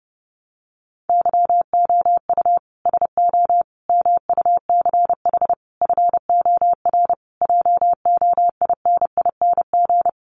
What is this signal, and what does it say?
Morse code, heard as a keyed tone, about 30 wpm, 700 Hz. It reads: YOU SO MUCH FOR JOINING
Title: Sets of 5 Words Encore2 30wpm